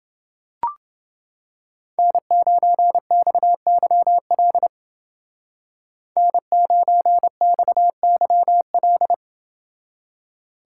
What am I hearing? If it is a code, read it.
Morse code, heard as a keyed tone, about 28 wpm, 700 Hz. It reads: N9XYL N9XYL